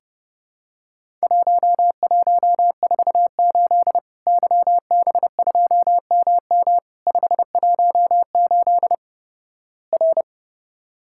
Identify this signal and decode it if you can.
Morse code, heard as a keyed tone, about 32 wpm, 700 Hz. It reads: 1148 YB2MM 518 R
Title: Worked All Europe DX Contest - QTC Exchange - 30wpm